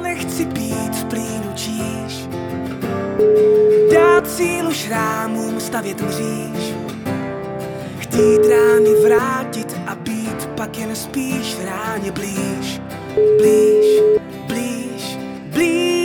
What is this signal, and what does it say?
Ringback tone (Europe, 425 Hz), with noise.